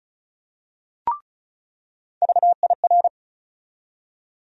Morse code, heard as a keyed tone, about 36 wpm, 700 Hz. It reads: VIR
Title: USA Call Sign Postfixes - 3 Letters 35wpm